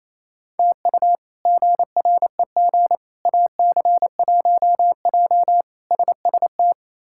Morse code, heard as a keyed tone, about 28 wpm, 700 Hz. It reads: TU GREG AC1J SST
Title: K1USN SST QSO Practice - 28wpm